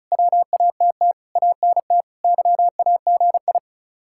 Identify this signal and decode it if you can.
Morse code, heard as a keyed tone, about 44 wpm, 700 Hz. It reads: WATT ANT YAGI